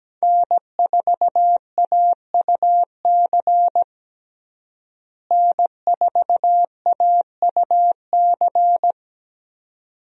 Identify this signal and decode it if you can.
Morse code, heard as a keyed tone, about 17 wpm, 700 Hz. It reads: N4AUC N4AUC
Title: USA Call Signs - 5 Characters - Repeated 2x - 17wpm